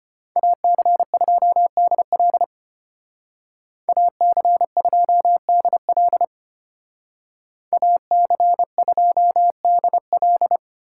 Morse code, heard as a keyed tone, about 30 wpm, 700 Hz. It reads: AC2DL AC2DL AC2DL
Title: Speed Racing - USA Call Signs 25wpm